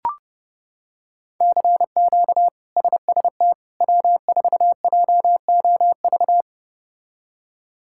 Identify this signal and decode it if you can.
Morse code, heard as a keyed tone, about 30 wpm, 700 Hz. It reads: CQ SST W4JOV
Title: K1USN SST QSO Practice - 30wpm